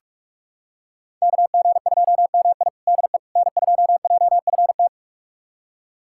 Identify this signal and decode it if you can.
Morse code, heard as a keyed tone, about 46 wpm, 700 Hz. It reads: KG2MI DE N2JFT
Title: USA Callsign DE Callsign 45wpm